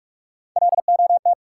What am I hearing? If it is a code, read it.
Morse code, heard as a keyed tone, about 45 wpm, 700 Hz. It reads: LOT
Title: Sentences from Top 500 Words - New Word Drill - 45wpm